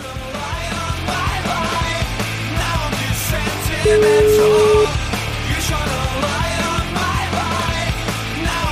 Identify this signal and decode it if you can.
Ringback tone (Europe, 425 Hz), with noise.